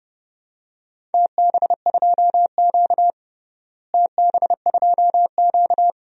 Morse code, heard as a keyed tone, about 31 wpm, 700 Hz. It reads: TB2Q TB2Q
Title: DXCC-100 Call Signs - 4 Characters - Repeated 3x - 30wpm